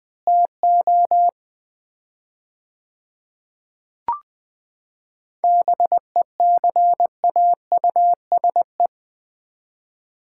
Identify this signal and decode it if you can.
Morse code, heard as a keyed tone, about 21 wpm, 700 Hz. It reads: TO BECAUSE
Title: 4x Word-Spacing - Sets of 3 Words - 20wpm